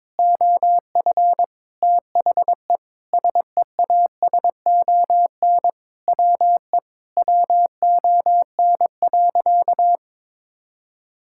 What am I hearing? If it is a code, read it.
Morse code, heard as a keyed tone, about 22 wpm, 700 Hz. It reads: OF THE SEASON WE WON.
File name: Sentences with 7-Word Phrases - 22wpm